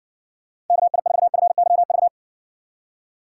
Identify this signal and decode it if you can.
Morse code, heard as a keyed tone, about 60 wpm, 700 Hz. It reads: KI4FYV